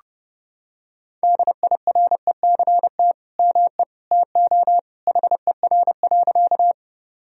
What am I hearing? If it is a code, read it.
Morse code, heard as a keyed tone, about 30 wpm, 700 Hz. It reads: DIRECT ME TO HER.
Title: Sentences from Top 1000 Words - Review Part 2 - 30wpm